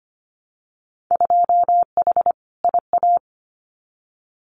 Morse code, heard as a keyed tone, about 25 wpm, 700 Hz. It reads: 2H IA
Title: Winter Field Day Exchanges Encore 25wpm